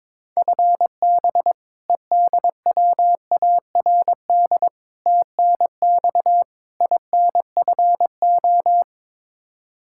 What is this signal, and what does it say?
Morse code, heard as a keyed tone, about 22 wpm, 700 Hz. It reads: FB EDWARD TNX INFO
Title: Practice QSO - Exchange 3 - More Info - 22wpm